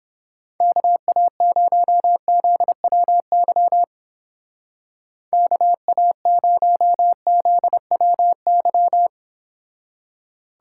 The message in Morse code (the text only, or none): KA0ZWY KA0ZWY